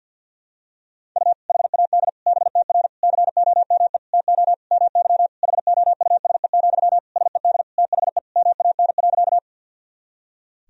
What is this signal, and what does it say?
Morse code, heard as a keyed tone, about 48 wpm, 700 Hz. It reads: U HAD BTR COME TO MY HOUSE, SED THE MAN.
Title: Sentences from Top 400 Words and Abbreviations 50wpm